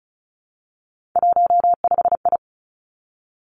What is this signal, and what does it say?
Morse code, heard as a keyed tone, about 36 wpm, 700 Hz. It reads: 15I
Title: Three Challenging Letter-Number Combinations 35wpm